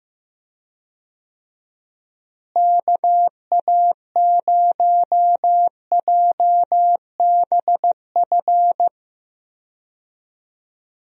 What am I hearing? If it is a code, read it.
Morse code, heard as a keyed tone, about 15 wpm, 700 Hz. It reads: KA0JBF
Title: USA Call Signs - 6 Characters - Repeated 3x - 15wpm